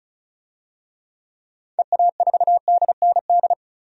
Morse code, heard as a keyed tone, about 34 wpm, 700 Hz. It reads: EA4DND
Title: DXCC-100 Call Signs - 6 Characters - 35wpm